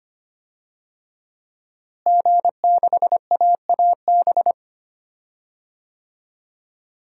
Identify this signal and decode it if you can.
Morse code, heard as a keyed tone, about 25 wpm, 700 Hz. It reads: G6AAB
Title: DXCC-100 Call Signs - 5 Characters - Repeated 2x - 25wpm